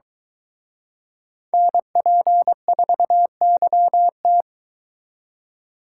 Morse code, heard as a keyed tone, about 23 wpm, 700 Hz. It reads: NP4YT